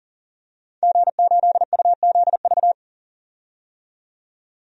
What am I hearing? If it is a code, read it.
Morse code, heard as a keyed tone, about 37 wpm, 700 Hz. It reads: G8UZV
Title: DXCC-100 Call Signs - 5 Characters - 40wpm